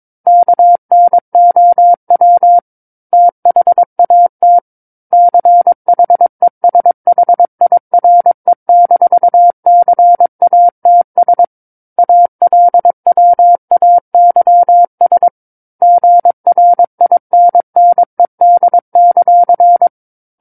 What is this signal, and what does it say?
Morse code, heard as a keyed tone, about 22 wpm, 700 Hz. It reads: KNOW THAT CHESHIRE-CATS ALWAYS GRINNED;
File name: 06-Chapter-06 22 WPM